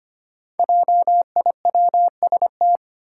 Morse code, heard as a keyed tone, about 25 wpm, 700 Hz. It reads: JIWST
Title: Five Challenging Letter-Number Combinations 25wpm